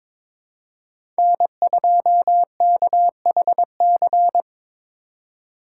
Morse code, heard as a keyed tone, about 22 wpm, 700 Hz. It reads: N2KHC
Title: USA Call Signs - 5 Characters - 3x Character Spacing - 22wpm